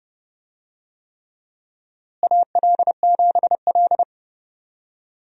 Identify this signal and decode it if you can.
Morse code, heard as a keyed tone, about 32 wpm, 700 Hz. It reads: AL7L
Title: USA Call Signs - 4 Characters - Repeated 3x - 30wpm